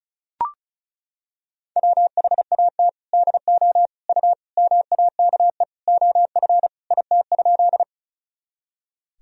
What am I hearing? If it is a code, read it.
Morse code, heard as a keyed tone, about 36 wpm, 700 Hz. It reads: WHAT DO U MAKE OF IT?
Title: Sentences from Top 100 Words and Abbreviations 35wpm